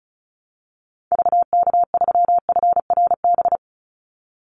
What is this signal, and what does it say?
Morse code, heard as a keyed tone, about 36 wpm, 700 Hz. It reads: VK3FRB
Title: USA Call Signs 35wpm